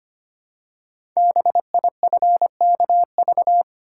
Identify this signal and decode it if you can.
Morse code, heard as a keyed tone, about 25 wpm, 700 Hz. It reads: BIFKV
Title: Code Groups - 5 Letter:Numbers - 25wpm